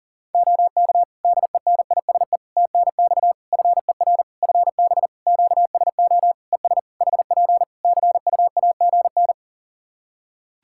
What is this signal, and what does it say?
Morse code, heard as a keyed tone, about 40 wpm, 700 Hz. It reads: OK DENISE TNX FER FB QSO ES HP CUAGN
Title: K1USN SST QSO Practice 40wpm